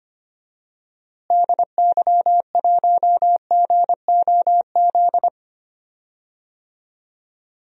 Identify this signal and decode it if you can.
Morse code, heard as a keyed tone, about 25 wpm, 700 Hz. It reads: DY1GOZ
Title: DXCC-100 Call Signs - 6 Characters - Repeated 2x - 25wpm